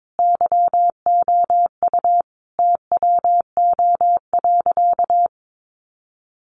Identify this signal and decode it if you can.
Morse code, heard as a keyed tone, about 22 wpm, 700 Hz. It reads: YOU TWO.
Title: Sentences from Top 1200 Words 22wpm